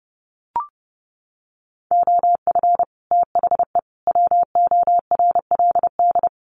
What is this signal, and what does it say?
Morse code, heard as a keyed tone, about 30 wpm, 700 Hz. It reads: OF THE WORLD
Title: Sets of 3 Words 30wpm